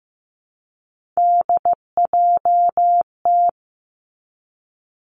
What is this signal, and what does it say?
Morse code, heard as a keyed tone, about 15 wpm, 700 Hz. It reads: DJT
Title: Three Challenging Letter-Number Combinations 15wpm